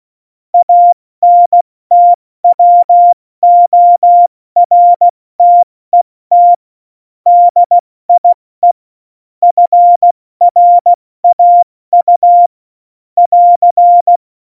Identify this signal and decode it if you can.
Morse code, heard as a keyed tone, about 16 wpm, 700 Hz. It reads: ANTWORTET DIE FRAU <AR>
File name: Morsekurs / Witze / Witz 27. August 2019, 70 F80